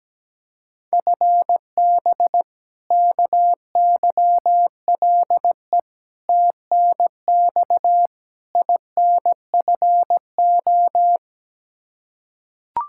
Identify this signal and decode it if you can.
Morse code, heard as a keyed tone, about 17 wpm, 700 Hz. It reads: FB KYLE TNX INFO
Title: Practice QSO - Exchange 3 - More Info - 17wpm